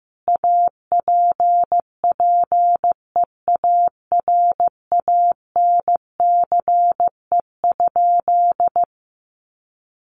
Morse code, heard as a keyed tone, about 14 wpm, 700 Hz. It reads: APPEARANCE?
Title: Sentences from Top 1100 Words 15wpm